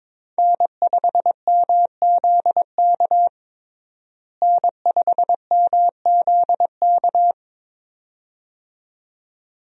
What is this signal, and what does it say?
Morse code, heard as a keyed tone, about 22 wpm, 700 Hz. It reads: N5MZK N5MZK
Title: USA Call Signs - 5 Characters - Repeated 2x - 22wpm